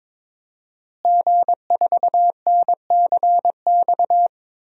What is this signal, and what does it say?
Morse code, heard as a keyed tone, about 22 wpm, 700 Hz. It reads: G4NCX
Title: DXCC-100 Call Signs - 5 Characters - Repeated 2x - 22wpm